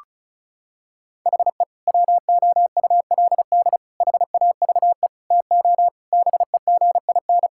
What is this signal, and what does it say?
Morse code, heard as a keyed tone, about 36 wpm, 700 Hz. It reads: HE WOULD HAVE TO BEGIN
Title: Sentences from Top 1000 Words - Review Part 5 - 35wpm